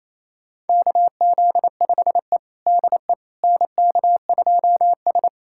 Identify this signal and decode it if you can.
Morse code, heard as a keyed tone, about 27 wpm, 690 Hz. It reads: KZ5E DE NK2S
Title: USA Callsign DE Callsign - 4 Characters - 28wpm